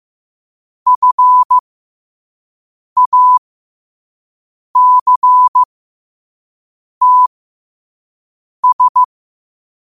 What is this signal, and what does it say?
Morse code, heard as a keyed tone, about 15 wpm, spaced as for 5 wpm, 1000 Hz. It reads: FACTS
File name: qotdfiles / 5wpm